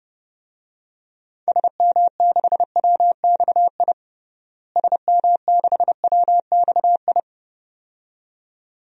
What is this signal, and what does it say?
Morse code, heard as a keyed tone, about 32 wpm, 700 Hz. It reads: SM6WXI SM6WXI
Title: DXCC-100 Call Signs - 6 Characters - Repeated 2x - 30wpm